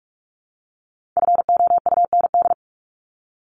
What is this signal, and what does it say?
Morse code, heard as a keyed tone, about 44 wpm, 700 Hz. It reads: FOUND
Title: Top 500 Words Repeated 45wpm